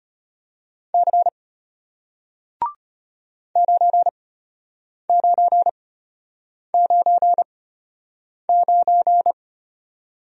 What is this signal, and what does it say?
Morse code, heard as a keyed tone, about 33 wpm, spaced as for 8 wpm, 700 Hz. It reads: C 9999